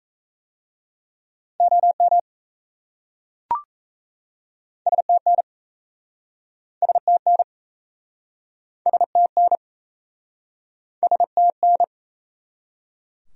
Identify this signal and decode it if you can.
Morse code, heard as a keyed tone, about 38 wpm, 700 Hz. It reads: OM STN STN STN STN